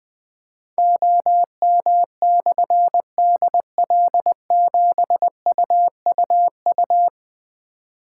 Morse code, heard as a keyed tone, about 20 wpm, 700 Hz. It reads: OM/DL7UUU
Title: Difficult Call Signs 20wpm